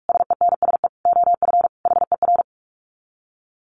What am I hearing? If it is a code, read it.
Morse code, heard as a keyed tone, about 44 wpm, 700 Hz. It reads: SENSE OF HER
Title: Dit Challenge 45wpm